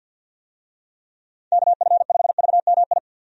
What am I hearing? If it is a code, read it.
Morse code, heard as a keyed tone, about 51 wpm, 700 Hz. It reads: KF5VMI
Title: USA Call Signs - 6 Characters - 50wpm